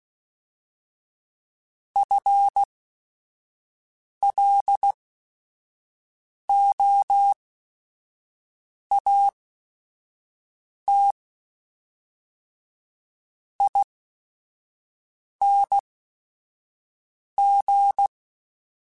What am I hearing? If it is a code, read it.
Morse code, heard as a keyed tone, about 16 wpm, spaced as for 5 wpm, 800 Hz. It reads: FLOAT ING
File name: FL / FL mp3s / Fast FL @5